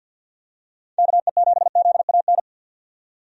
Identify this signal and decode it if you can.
Morse code, heard as a keyed tone, about 47 wpm, 700 Hz. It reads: KE7ZAN
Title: USA Call Signs - 6 Characters - 50wpm